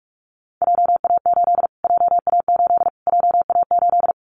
Morse code, heard as a keyed tone, about 44 wpm, 700 Hz. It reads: JA8 JA8 JA8
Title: Callsign Prefixes for Top 25 DXCC - Repeated 3x - 45wpm